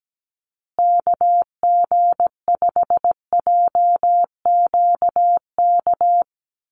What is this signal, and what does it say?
Morse code, heard as a keyed tone, about 17 wpm, 700 Hz. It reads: KG5JQK
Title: USA Call Signs Repeated 2x 17wpm